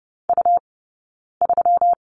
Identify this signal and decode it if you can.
Morse code, heard as a keyed tone, about 30 wpm, 700 Hz. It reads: U 3
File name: Lesson 20d - Number 3 - Two Characters 30wpm